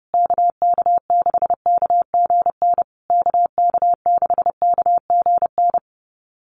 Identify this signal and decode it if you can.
Morse code, heard as a keyed tone, about 30 wpm, 700 Hz. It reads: KK6KGN KK6KGN
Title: USA Call Signs Repeated 3x 30wpm